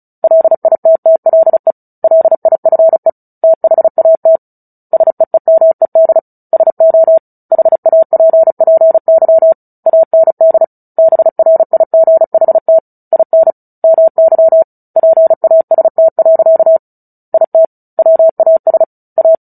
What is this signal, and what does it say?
Morse code, heard as a keyed tone, about 34 wpm, 650 Hz. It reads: LITTLE LIFE THAT SEEMED SO HAPPY AND BRIGHT IN MY PAST. IT WAS A
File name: Book Two / 09-Chapter 9 - 35 WPM